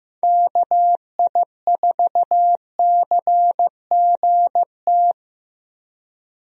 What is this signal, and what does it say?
Morse code, heard as a keyed tone, about 15 wpm, 700 Hz. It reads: KI4CGT
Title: USA Call Signs - 6 Characters - Repeated 2x - 15wpm